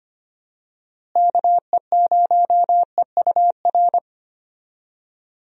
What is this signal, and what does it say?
Morse code, heard as a keyed tone, about 25 wpm, 700 Hz. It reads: KE0EUR
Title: USA Call Signs - 6 Characters - Repeated 2x - 25wpm